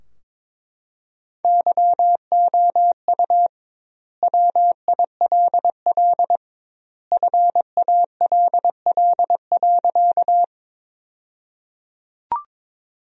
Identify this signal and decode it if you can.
Morse code, heard as a keyed tone, about 22 wpm, 700 Hz. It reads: YOU WILL FALL.
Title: 2x Word-Spacing - Sentences from Top 500 Words - 22wpm